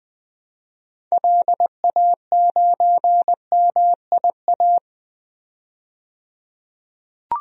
Morse code, heard as a keyed tone, about 20 wpm, 700 Hz. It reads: LA9MIA
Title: DXCC-100 Call Signs - 6 Characters - 20wpm